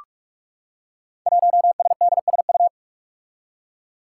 Morse code, heard as a keyed tone, about 44 wpm, 700 Hz. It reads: 1SDSU
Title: Five Challenging Letter-Number Combinations 45wpm